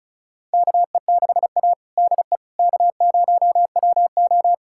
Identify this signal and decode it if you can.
Morse code, heard as a keyed tone, about 34 wpm, 700 Hz. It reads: KE6A DE K0WO
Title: USA Callsign DE Callsign - 4 Characters - 35wpm